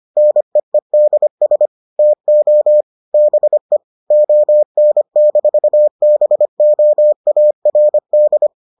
Morse code, heard as a keyed tone, about 25 wpm, 600 Hz. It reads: NEEDS TO BE ON-BOARD